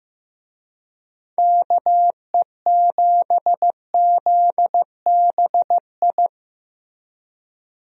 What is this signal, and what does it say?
Morse code, heard as a keyed tone, about 15 wpm, 700 Hz. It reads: KE7ZBI
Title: USA Call Signs - 6 Characters - Repeated 2x - 15wpm